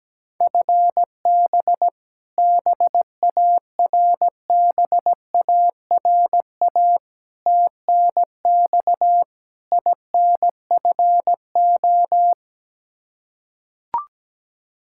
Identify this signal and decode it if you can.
Morse code, heard as a keyed tone, about 17 wpm, 700 Hz. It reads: FB BARBARA TNX INFO
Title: Practice QSO - Exchange 3 - More Info - 17wpm